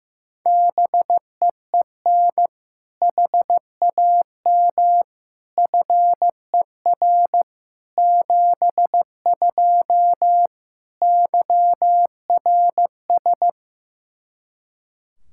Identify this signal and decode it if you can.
Morse code, heard as a keyed tone, about 15 wpm, 700 Hz. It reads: BEEN HAM FER 72 YRS
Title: Practice QSO - Exchange 3 - More Info - 15wpm